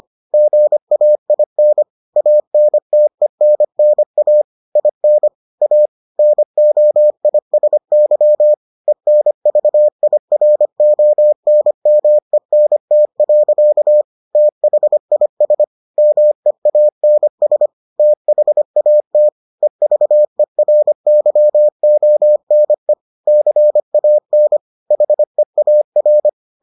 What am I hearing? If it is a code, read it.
Morse code, heard as a keyed tone, about 25 wpm, 600 Hz. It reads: GAIN ANTENNA IN A NOISY ENVIRONMENT. THIS MEANS THAT EVERYONE CAN HEAR